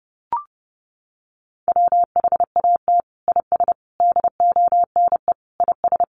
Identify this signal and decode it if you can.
Morse code, heard as a keyed tone, about 30 wpm, 700 Hz. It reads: WHAT IS DONE IS
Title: Sentences from Top 500 Words 30wpm